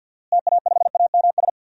Morse code, heard as a keyed tone, about 53 wpm, 700 Hz. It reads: TA5AMS
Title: DXCC-100 Call Signs - 6 Characters - Repeated 3x - 50wpm